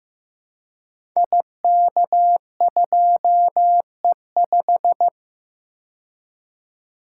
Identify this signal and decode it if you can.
Morse code, heard as a keyed tone, about 15 wpm, 700 Hz. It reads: IK2E5
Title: Code Groups - 5 Letter:Numbers - 15wpm